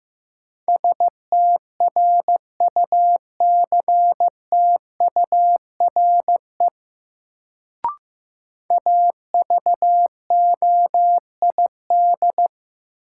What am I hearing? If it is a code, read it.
Morse code, heard as a keyed tone, about 15 wpm, 700 Hz. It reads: STRUCTURE AVOID
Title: Sentences from Top 1000 Words - Review Part 6 - New Word Drill - 15wpm